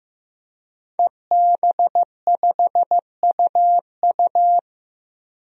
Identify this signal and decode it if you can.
Morse code, heard as a keyed tone, about 15 wpm, 700 Hz. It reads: EB5UU
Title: DXCC-100 Call Signs - 5 Characters - 15wpm